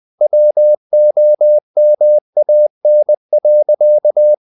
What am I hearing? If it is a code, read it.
Morse code, heard as a keyed tone, about 20 wpm, 600 Hz. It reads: WOMAN.
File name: the old man and the sea / the old man and the sea part 1 20wpm